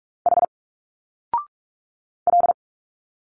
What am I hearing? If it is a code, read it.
Morse code, heard as a keyed tone, about 43 wpm, 700 Hz. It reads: H L
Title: Single Letters 45wpm